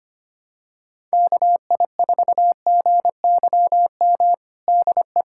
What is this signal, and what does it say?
Morse code, heard as a keyed tone, about 25 wpm, 700 Hz. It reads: KI4GYM DE
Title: USA Callsign DE Callsign - 6 Characters - 25wpm